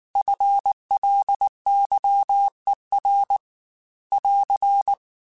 Morse code, heard as a keyed tone, about 19 wpm, 800 Hz. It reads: FLYER <AR>
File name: FL Runs / FL Runs mp3s / FL Runs @18